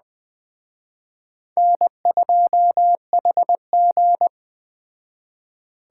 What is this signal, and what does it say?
Morse code, heard as a keyed tone, about 20 wpm, 700 Hz. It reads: N2HG
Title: Speed Racing - USA 1x2 Call Signs 17wpm